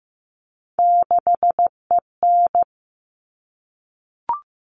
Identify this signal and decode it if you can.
Morse code, heard as a keyed tone, about 15 wpm, 700 Hz. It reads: 6EN
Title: All Cut Numbers - 3-Digit 15wpm